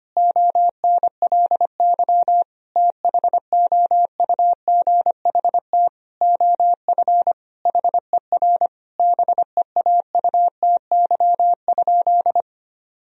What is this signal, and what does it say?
Morse code, heard as a keyed tone, about 25 wpm, 700 Hz. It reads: ONLY THOUGHT OF HER BEAUTY?
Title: Sentences from Top 1000 Words - Review Part 2 - 25wpm